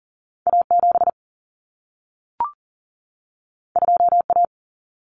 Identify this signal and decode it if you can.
Morse code, heard as a keyed tone, about 40 wpm, 700 Hz. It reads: A7 2A
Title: Common Cut Numbers - 2-Digit 40wpm